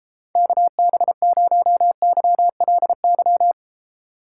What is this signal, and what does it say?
Morse code, heard as a keyed tone, about 33 wpm, 700 Hz. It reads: KB0YLY